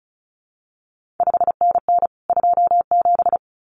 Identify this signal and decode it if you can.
Morse code, heard as a keyed tone, about 36 wpm, 700 Hz. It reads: 5NN 27
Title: CQ WW DX Contest Exchange Encore - 35wpm